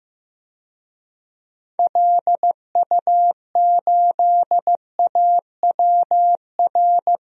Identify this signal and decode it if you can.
Morse code, heard as a keyed tone, about 15 wpm, 700 Hz. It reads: LU8AWR
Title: DXCC-100 Call Signs - 6 Characters - 15wpm